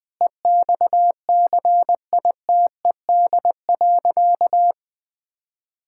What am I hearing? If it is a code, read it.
Morse code, heard as a keyed tone, about 20 wpm, 700 Hz. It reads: EXCITED.
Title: Sentences with 5-Word Phrases - 20wpm